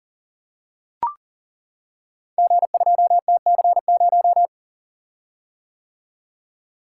Morse code, heard as a keyed tone, about 41 wpm, 700 Hz. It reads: G2TC0
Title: Code Groups - 5 Letter:Numbers - 40wpm